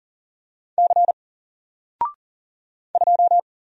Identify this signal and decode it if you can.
Morse code, heard as a keyed tone, about 40 wpm, 700 Hz. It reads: C 2
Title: Single Letter-Number Alt 40wpm